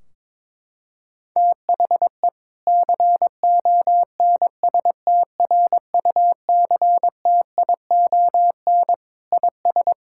Morse code, heard as a keyed tone, about 22 wpm, 700 Hz. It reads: THE CONSTRUCTION IS
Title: Sentences from Top 1600 Words 22wpm